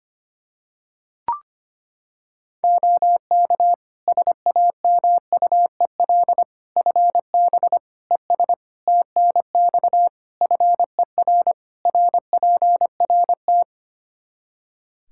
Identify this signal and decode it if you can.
Morse code, heard as a keyed tone, about 25 wpm, 700 Hz. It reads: OK SAMUEL FB ES TNX FER RPRT
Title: Practice QSO - Exchange 2 - Info - 25wpm